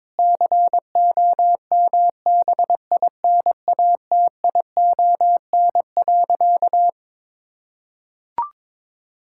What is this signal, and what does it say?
Morse code, heard as a keyed tone, about 22 wpm, 700 Hz. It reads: COMBINATION.
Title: Sentences from Top 1000 Words - Review Part 6 - 22wpm